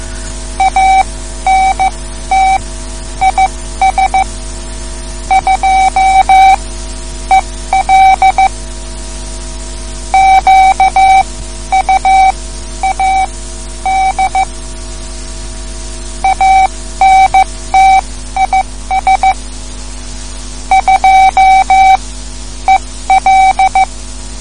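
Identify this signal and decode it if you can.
Morse code, heard as a keyed tone, about 15 wpm, spaced as for 11 wpm, 770 Hz. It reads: ANT IS 2EL QUAD ANT IS 2EL
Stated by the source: Morse code lesson, third lesson in a series for learning the Morse alphabet.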